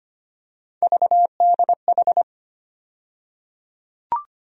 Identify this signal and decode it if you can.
Morse code, heard as a keyed tone, about 25 wpm, 700 Hz. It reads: VDH
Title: USA Call Sign Postfixes - 3 Letters 25wpm